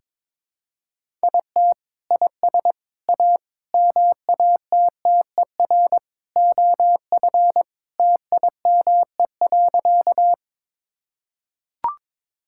Morse code, heard as a keyed tone, about 22 wpm, 700 Hz. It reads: IT IS A MATTER OF TIME.
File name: practice / 22wpm / Sentences with 3-Word Phrases - 22wpm